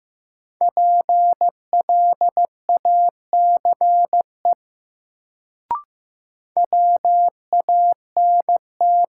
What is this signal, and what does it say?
Morse code, heard as a keyed tone, about 15 wpm, 700 Hz. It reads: PLACE WANT
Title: Sentences from Top 200 Words - New Word Drill - 15wpm